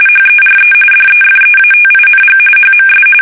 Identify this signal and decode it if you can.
RTTY, about 45.45 baud. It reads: HERE DEFEATING HMS
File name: fsk850 / usnavy1